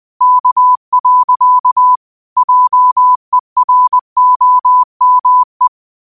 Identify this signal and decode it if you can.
Morse code, heard as a keyed tone, about 20 wpm, 1000 Hz. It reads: K. JEROME
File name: qotdfiles / 20wpm